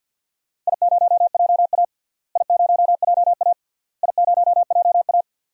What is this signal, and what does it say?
Morse code, heard as a keyed tone, about 50 wpm, 700 Hz. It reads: I0JA I0JA I0JA
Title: DXCC-100 Call Signs - 4 Characters - Repeated 3x - 50wpm